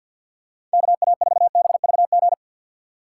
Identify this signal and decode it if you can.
Morse code, heard as a keyed tone, about 52 wpm, 700 Hz. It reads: KA4BVG
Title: K1USN SST QSO Practice - 50wpm